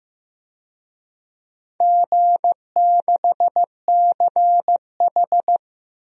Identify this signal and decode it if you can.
Morse code, heard as a keyed tone, about 15 wpm, 700 Hz. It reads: G6CH
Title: DXCC-100 Call Signs - 4 Characters - Repeated 2x - 15wpm